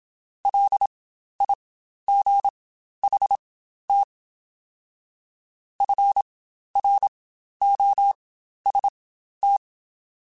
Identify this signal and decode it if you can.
Morse code, heard as a keyed tone, about 27 wpm, spaced as for 12 wpm, 800 Hz. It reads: LIGHT FROST
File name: Spaced Characters / Spaced Characters-mp3s / Weather4 24 sp @25